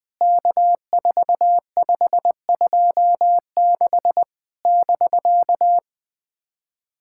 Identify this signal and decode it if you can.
Morse code, heard as a keyed tone, about 19 wpm, 700 Hz. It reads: K4526 <BK>
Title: POTA QSO Practice - 20wpm